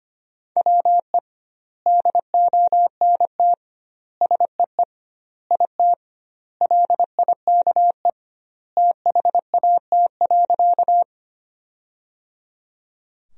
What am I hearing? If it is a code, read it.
Morse code, heard as a keyed tone, about 25 wpm, 700 Hz. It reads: WE DONT SEE IT LIKE THAT.
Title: 2x Word-Spacing - Sentences from Top 300 Words - 25wpm